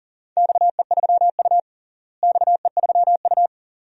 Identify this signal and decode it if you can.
Morse code, heard as a keyed tone, about 40 wpm, 700 Hz. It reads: XE3U XE3U
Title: DXCC-100 Call Signs - 4 Characters - Repeated 2x - 40wpm